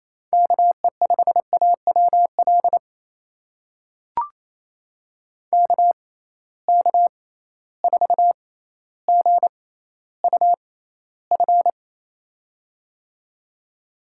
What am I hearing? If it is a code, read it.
Morse code, heard as a keyed tone, about 28 wpm, spaced as for 14 wpm, 700 Hz. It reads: KE5AWL KK4GUF
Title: USA Call Signs - 6 Characters - 6x Character Spacing - 28wpm